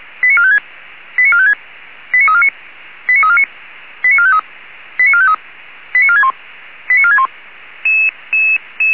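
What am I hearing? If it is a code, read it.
Five-tone selective call (ZVEI), heard as quick runs of five tones, tones of about 80 ms. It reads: 89456 89456 89349 89349 89453 89453 89461 89461
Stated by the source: ZVEI five-tone selective call received on an SDR.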